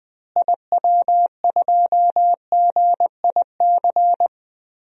Morse code, heard as a keyed tone, about 20 wpm, 700 Hz. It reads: IW2GIC
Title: DXCC-100 Call Signs - 6 Characters - Repeated 2x - 20wpm